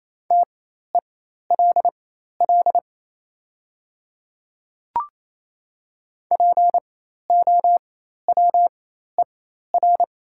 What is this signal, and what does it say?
Morse code, heard as a keyed tone, about 27 wpm, spaced as for 13 wpm, 700 Hz. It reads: TELL POWER
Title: 4x Character Spacing - Top 500 Words - 28wpm